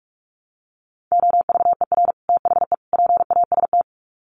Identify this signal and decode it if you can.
Morse code, heard as a keyed tone, about 44 wpm, 700 Hz. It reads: OVER THE PAST
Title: Sets of 3 Words 45wpm